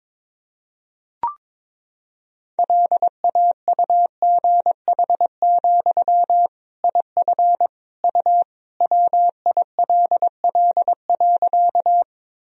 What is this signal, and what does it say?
Morse code, heard as a keyed tone, about 22 wpm, 700 Hz. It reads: LAUGH, IF U WILL.
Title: Sentences from Top 500 Words and Abbreviations 22wpm